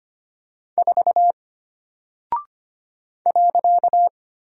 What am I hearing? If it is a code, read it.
Morse code, heard as a keyed tone, about 25 wpm, 700 Hz. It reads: 4 .
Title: Single Character (Letter:Number:Punctuation:Prosign) Alt 25wpm